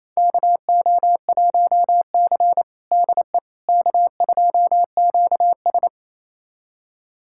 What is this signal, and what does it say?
Morse code, heard as a keyed tone, about 27 wpm, 690 Hz. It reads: KO1C DE K2QS
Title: USA Callsign DE Callsign - 4 Characters - 28wpm